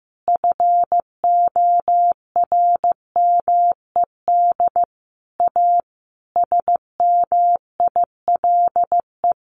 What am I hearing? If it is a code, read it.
Morse code, heard as a keyed tone, about 15 wpm, 700 Hz. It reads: FORMED A SMILE
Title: A Sound Of Thunder 15wpm